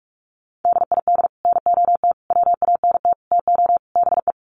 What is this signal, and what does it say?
Morse code, heard as a keyed tone, about 44 wpm, 700 Hz. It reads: DID NOT WANT TO BE
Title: Sets of 5 Words 45wpm